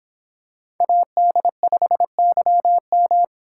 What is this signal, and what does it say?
Morse code, heard as a keyed tone, about 25 wpm, 700 Hz. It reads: AD5YM